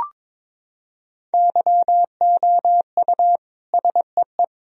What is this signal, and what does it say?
Morse code, heard as a keyed tone, about 22 wpm, 700 Hz. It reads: YOU SEE
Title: Sentences from Top 1000 Words - Review Part 1 - 22wpm